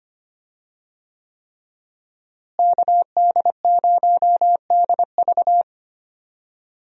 Morse code, heard as a keyed tone, about 25 wpm, 700 Hz. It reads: KD0DV
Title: USA Call Signs - 5 Characters - Repeated 3x - 25wpm